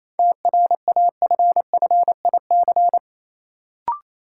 Morse code, heard as a keyed tone, about 28 wpm, 700 Hz. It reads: TRAFFIC
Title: Sentences from Top 1300 Words - New Word Drill Alt - 28wpm